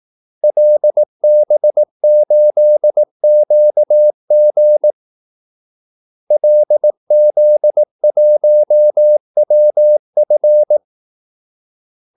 Morse code, heard as a keyed tone, about 18 wpm, 600 Hz. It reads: LB8QG LZ1WF
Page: Kaldesignaler 18 wpm